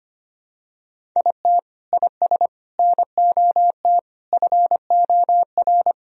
Morse code, heard as a keyed tone, about 25 wpm, 700 Hz. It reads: IT IS NOT FOR
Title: Sentences from Top 1000 Words - Review Part 1 - 25wpm